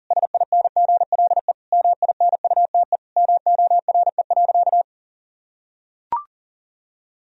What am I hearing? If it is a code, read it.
Morse code, heard as a keyed tone, about 40 wpm, 700 Hz. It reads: SINGLE MINUTE MORE.
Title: Sentences from Top 1000 Words - Review Part 4 - 40wpm